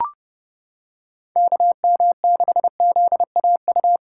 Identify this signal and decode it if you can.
Morse code, heard as a keyed tone, about 30 wpm, 700 Hz. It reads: KM6ZAU